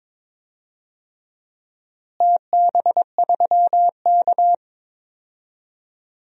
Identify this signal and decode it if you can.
Morse code, heard as a keyed tone, about 21 wpm, 700 Hz. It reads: TB3K
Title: DXCC-100 Call Signs - 4 Characters - Repeated 3x - 22wpm